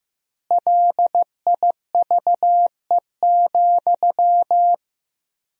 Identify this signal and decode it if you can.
Morse code, heard as a keyed tone, about 15 wpm, 700 Hz. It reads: LIVE,
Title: 3x Word-Spacing - Sentences from Top 200 Words - 15wpm